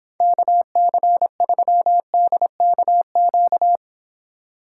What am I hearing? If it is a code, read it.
Morse code, heard as a keyed tone, about 26 wpm, 700 Hz. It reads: KC3DKQ